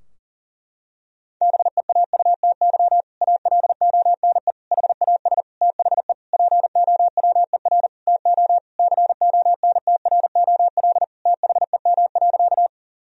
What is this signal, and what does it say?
Morse code, heard as a keyed tone, about 37 wpm, 700 Hz. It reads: BEAUTY ALONE HAS THE POWER TO CONTROL THEM.
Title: Sentences from Top 1000 Words - Review Part 2 - 40wpm